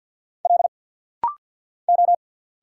Morse code, heard as a keyed tone, about 48 wpm, 700 Hz. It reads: L O
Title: Single Letters - Rapid-Fire 50wpm